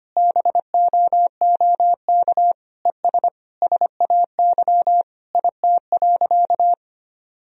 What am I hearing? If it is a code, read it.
Morse code, heard as a keyed tone, about 25 wpm, 700 Hz. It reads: BOOK ES SAY IT.
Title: Sentences from Top 400 Words and Abbreviations 25wpm